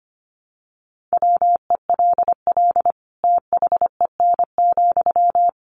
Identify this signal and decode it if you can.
Morse code, heard as a keyed tone, about 25 wpm, 700 Hz. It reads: WELL THEN,
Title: Sentences from Top 500 Words 25wpm